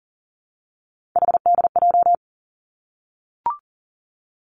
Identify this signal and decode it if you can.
Morse code, heard as a keyed tone, about 41 wpm, 700 Hz. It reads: HDJ
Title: Three Challenging Letter-Number Combinations 40wpm